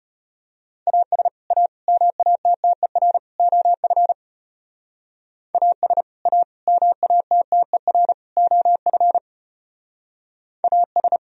Morse code, heard as a keyed tone, about 31 wpm, 700 Hz. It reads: AS A MATTER OF AS A MATTER OF AS
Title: Speed-Racing - Sets of 4 Words 25wpm